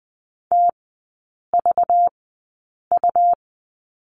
Morse code, heard as a keyed tone, about 20 wpm, spaced as for 8 wpm, 700 Hz. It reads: TVU
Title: 212 Lesson 34e - Number 8 - Three Characters 20wpm